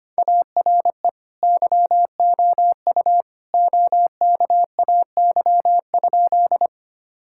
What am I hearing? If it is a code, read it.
Morse code, heard as a keyed tone, about 25 wpm, 700 Hz. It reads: ARE YOU OKAY?
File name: practice / 25wpm / Sentences with 2-Word Phrases - 25wpm